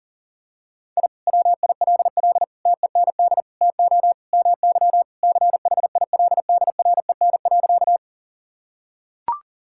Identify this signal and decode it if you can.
Morse code, heard as a keyed tone, about 40 wpm, 700 Hz. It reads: I WILL TEND TO MY CHILDREN.
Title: Sentences from Top 1000 Words - Review Part 5 - 40wpm